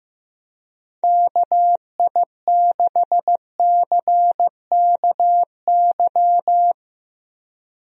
Morse code, heard as a keyed tone, about 15 wpm, 700 Hz. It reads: KI6CKY
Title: USA Call Signs - 6 Characters - Repeated 3x - 15wpm